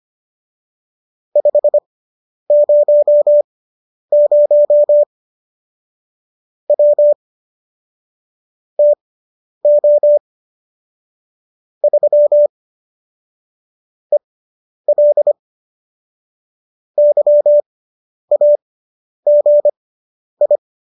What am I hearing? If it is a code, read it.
Morse code, heard as a keyed tone, about 25 wpm, spaced as for 10 wpm, 590 Hz. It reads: 500 W TO 3 EL YAGI
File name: qso005 10